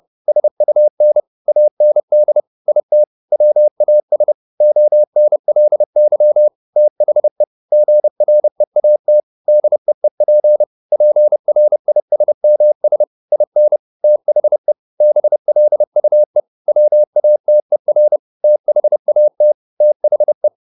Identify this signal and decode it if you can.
Morse code, heard as a keyed tone, about 30 wpm, 600 Hz. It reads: SUN AND IT WAS ONLY THE GREAT DEEP PRISMS IN THE BLUE WATER THAT THE
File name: the old man and the sea / the old man and the sea 1 30wpm